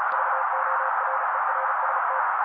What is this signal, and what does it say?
Morse code, heard as a keyed tone, about 35 wpm, 530 Hz. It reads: VORERST
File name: daily / 35wpm noise